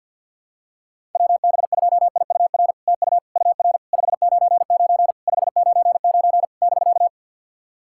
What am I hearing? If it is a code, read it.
Morse code, heard as a keyed tone, about 50 wpm, 700 Hz. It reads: WB2IUR TU UR 599 599 <BK>
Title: SOTA QSO Practice 50wpm